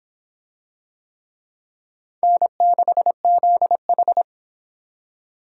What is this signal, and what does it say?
Morse code, heard as a keyed tone, about 22 wpm, 700 Hz. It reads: N6ZH